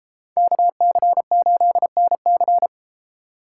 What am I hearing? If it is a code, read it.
Morse code, heard as a keyed tone, about 29 wpm, 700 Hz. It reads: KC8NC